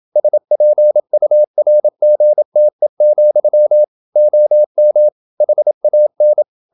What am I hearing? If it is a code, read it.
Morse code, heard as a keyed tone, about 27 wpm, 600 Hz. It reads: SPURGTE, OM HAN
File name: Ferie QRQ 27wpm